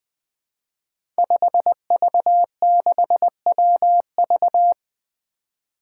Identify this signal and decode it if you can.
Morse code, heard as a keyed tone, about 20 wpm, 700 Hz. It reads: 5V6WV
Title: Five Challenging Letter-Number Combinations 20wpm